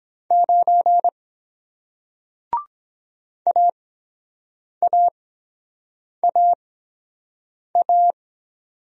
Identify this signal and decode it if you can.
Morse code, heard as a keyed tone, about 23 wpm, spaced as for 7 wpm, 700 Hz. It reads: 9 AAAA